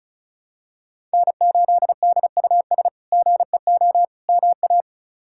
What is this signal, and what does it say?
Morse code, heard as a keyed tone, about 34 wpm, 700 Hz. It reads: N8DUS GEO MA
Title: K1USN SST QSO Practice - 35wpm